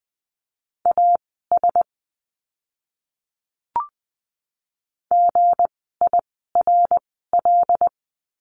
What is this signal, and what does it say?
Morse code, heard as a keyed tone, about 20 wpm, spaced as for 14 wpm, 700 Hz. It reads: AS GIRL
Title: 160 Lesson 26f - Letter g - Words 20wpm